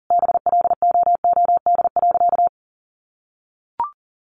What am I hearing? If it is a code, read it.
Morse code, heard as a keyed tone, about 40 wpm, 700 Hz. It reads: BLOOD.
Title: Sentences from Top 1200 Words 40wpm